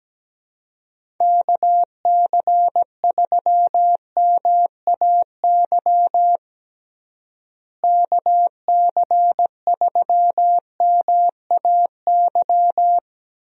Morse code, heard as a keyed tone, about 17 wpm, 700 Hz. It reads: KC3MAY KC3MAY
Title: USA Call Signs - 6 Characters - Repeated 3x - 17wpm